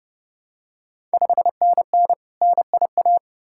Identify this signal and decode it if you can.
Morse code, heard as a keyed tone, about 30 wpm, 700 Hz. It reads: 5NN NIA
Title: Ontario QSO Party Exchange 30wpm